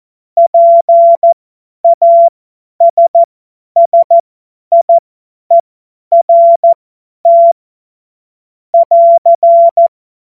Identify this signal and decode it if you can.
Morse code, heard as a keyed tone, about 14 wpm, spaced as for 10 wpm, 680 Hz. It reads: PASSIERT <AR>
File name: Morsekurs / Witze / Witz 01. August 2023 50 F70